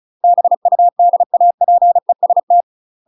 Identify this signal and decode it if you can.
Morse code, heard as a keyed tone, about 36 wpm, 700 Hz. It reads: BUDAPEST
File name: UK MASTER QSOs TRAINING/35WPM